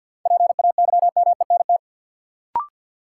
Morse code, heard as a keyed tone, about 50 wpm, 700 Hz. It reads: PAYMENT
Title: Sentences from Top 1200 Words - New Word Drill Alt - 50wpm